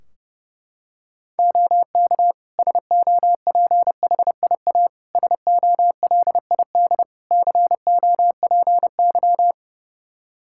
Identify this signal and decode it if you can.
Morse code, heard as a keyed tone, about 30 wpm, 700 Hz. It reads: OK SOPHIA SOLID COPY
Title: Practice QSO - Exchange 3 - More Info - 30wpm